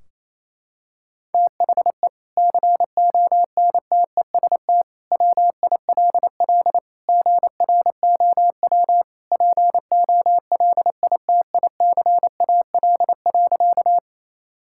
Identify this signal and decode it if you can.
Morse code, heard as a keyed tone, about 28 wpm, 700 Hz. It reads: THE CONTEST WILL GROW POLITICAL.
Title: Sentences from Top 1600 Words 28wpm